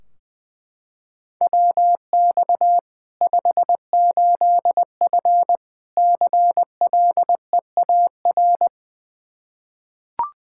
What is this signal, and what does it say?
Morse code, heard as a keyed tone, about 20 wpm, 700 Hz. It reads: WX 58F CLEAR
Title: Practice QSO - Exchange 2 - Info - 20wpm